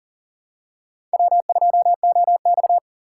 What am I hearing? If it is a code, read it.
Morse code, heard as a keyed tone, about 40 wpm, 700 Hz. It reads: W2OX
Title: USA Call Signs - 4 Characters - 6x Character Spacing - 40wpm